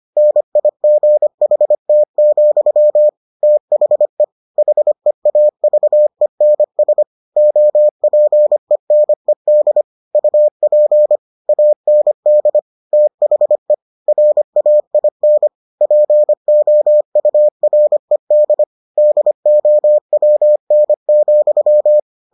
Morse code, heard as a keyed tone, about 25 wpm, 600 Hz. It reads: NIGHT, THE HEAVENS OPENED UP AND THE RAIN POURED DOWN,